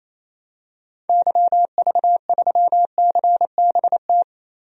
Morse code, heard as a keyed tone, about 28 wpm, 700 Hz. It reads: YV3CBT
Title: DXCC-100 Call Signs - 6 Characters - Repeated 3x - 28wpm